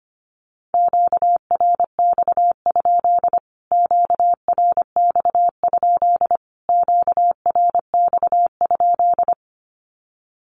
Morse code, heard as a keyed tone, about 25 wpm, 700 Hz. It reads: QRX? QRX? QRX?
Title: Q-Codes Repeated 3x 25wpm